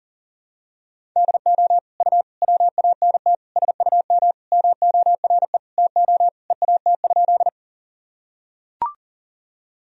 Morse code, heard as a keyed tone, about 40 wpm, 700 Hz. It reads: DO U WANT SUM MORE TO EAT?
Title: Sentences from Top 300 Words and Abbreviations 40wpm